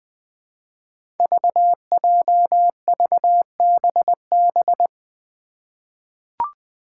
Morse code, heard as a keyed tone, about 20 wpm, 700 Hz. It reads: VJVBB
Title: Five Challenging Letter-Number Combinations 20wpm